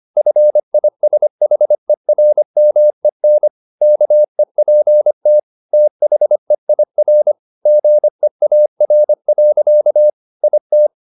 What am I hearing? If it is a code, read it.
Morse code, heard as a keyed tone, about 25 wpm, 600 Hz. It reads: FISHERMEN KEPT THEIR GEAR. IT
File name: the old man and the sea part 1 25wpm